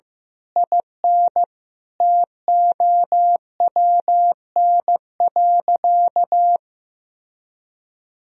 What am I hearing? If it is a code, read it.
Morse code, heard as a keyed tone, about 15 wpm, 700 Hz. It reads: IN TOWN.
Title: Sentences from Top 1000 Words - Review Part 4 - 15wpm